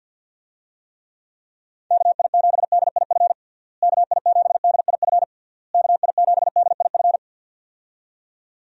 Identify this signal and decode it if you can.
Morse code, heard as a keyed tone, about 51 wpm, 700 Hz. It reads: KI7DIF KI7DIF KI7DIF
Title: USA Call Signs - 6 Characters - Repeated 3x - 50wpm